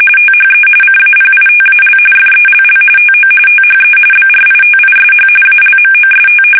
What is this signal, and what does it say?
RTTY, about 45.45 baud. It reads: THE CAPITAL FELL TO THE BRITISH ALMOST
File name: fsk850 / usnavy2